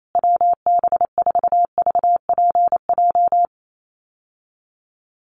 Morse code, heard as a keyed tone, about 28 wpm, 700 Hz. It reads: WB4VPJ
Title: USA Call Signs Repeated 2x 28wpm